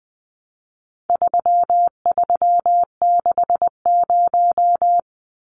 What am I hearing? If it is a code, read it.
Morse code, heard as a keyed tone, about 20 wpm, 700 Hz. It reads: 3360
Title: Numbers - 4-Digit 20wpm